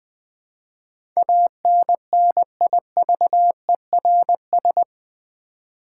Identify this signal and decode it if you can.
Morse code, heard as a keyed tone, about 20 wpm, 700 Hz. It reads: ANNIVERS
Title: Build A Word - 4 Letters At A Time - 11-Letter Words - 20wpm